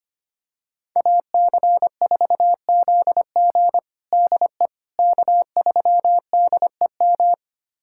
Morse code, heard as a keyed tone, about 25 wpm, 700 Hz. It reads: AC4ZG DE K3DEM
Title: USA Callsign DE Callsign - 5 Characters - 25wpm